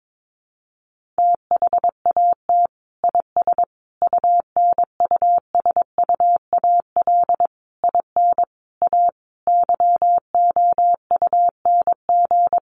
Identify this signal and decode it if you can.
Morse code, heard as a keyed tone, about 22 wpm, 700 Hz. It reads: THAT IS UNUSUAL IN A YOUNG
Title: Sentences from Top 1400 Words 22wpm